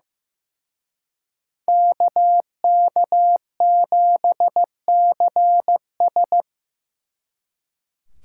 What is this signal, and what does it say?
Morse code, heard as a keyed tone, about 18 wpm, 700 Hz. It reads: KK7CS
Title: Speed Racing - USA Call Signs 15wpm